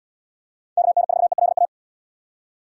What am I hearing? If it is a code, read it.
Morse code, heard as a keyed tone, about 73 wpm, 700 Hz. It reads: KN4ELA